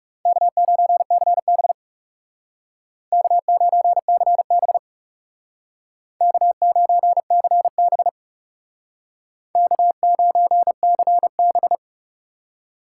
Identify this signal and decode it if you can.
Morse code, heard as a keyed tone, about 37 wpm, 700 Hz. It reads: K9CB K9CB K9CB K9CB
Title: Speed Racing - USA Call Signs 30wpm